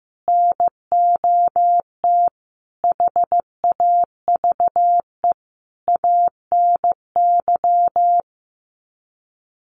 Morse code, heard as a keyed tone, about 15 wpm, 700 Hz. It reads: NOT HAVE ANY
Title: Sets of 5 Words Encore 15wpm